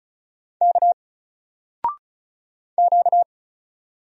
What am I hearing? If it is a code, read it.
Morse code, heard as a keyed tone, about 35 wpm, 700 Hz. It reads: K Q
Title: Single Letter-Number 35wpm